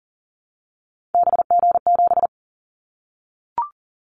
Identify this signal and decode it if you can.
Morse code, heard as a keyed tone, about 40 wpm, 700 Hz. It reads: BG7
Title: Callsign Prefixes for Top 100 DXCC - 40wpm